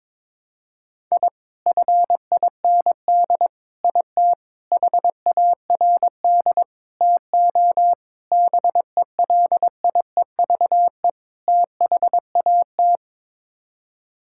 Morse code, heard as a keyed tone, about 21 wpm, 700 Hz. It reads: I FIND IT HARD TO BELIEVE THAT
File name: practice / 22wpm / Sets of 7 Words - 22wpm